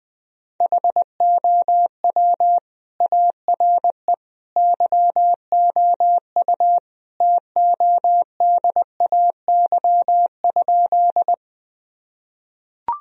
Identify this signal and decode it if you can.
Morse code, heard as a keyed tone, about 20 wpm, 700 Hz. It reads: HOW ARE YOU TODAY?
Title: Sentences from Top 1000 Words - Review Part 4 - 20wpm